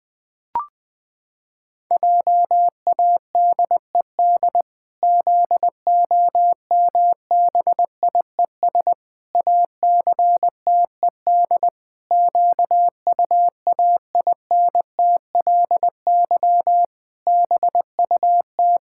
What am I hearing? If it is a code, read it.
Morse code, heard as a keyed tone, about 20 wpm, 700 Hz. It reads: JADED ZOMBIES ACTED QUAINTLY BUT
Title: Pangram Challenge v2 - 20wpm